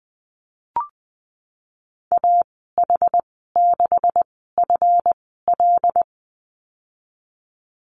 Morse code, heard as a keyed tone, about 19 wpm, spaced as for 14 wpm, 700 Hz. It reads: AH6FL
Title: 147 Lesson 24g - Letter y - Call Signs 20wpm